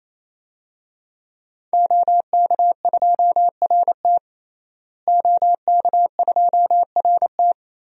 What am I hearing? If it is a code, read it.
Morse code, heard as a keyed tone, about 27 wpm, 700 Hz. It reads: OK2RT OK2RT
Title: DXCC-100 Call Signs - 5 Characters - Repeated 3x - 28wpm